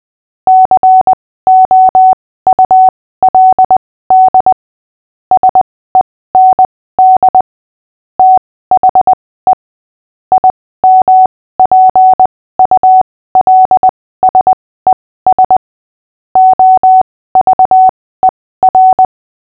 Morse code, heard as a keyed tone, about 20 wpm, spaced as for 15 wpm, 750 Hz. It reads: COULD SEND THE IMPULSES OVER
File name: cw / fists / Short-Story-6-15wpm